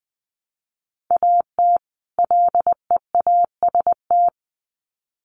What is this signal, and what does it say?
Morse code, heard as a keyed tone, about 20 wpm, 700 Hz. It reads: AT LEAST
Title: Sets of 2 Words 20wpm